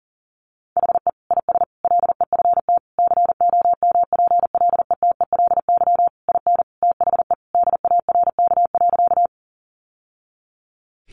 Morse code, heard as a keyed tone, about 40 wpm, 700 Hz. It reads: HE IS LEFT COMPLETELY IN THE DARK.
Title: Sentences from Top 1100 Words 40wpm